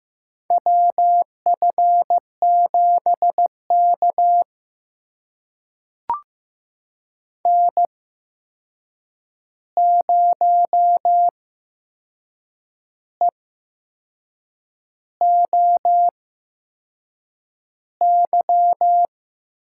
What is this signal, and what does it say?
Morse code, heard as a keyed tone, about 15 wpm, spaced as for 6 wpm, 700 Hz. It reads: WF7K N0EOY
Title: USA Call Signs - 8x Character Spacing - 15wpm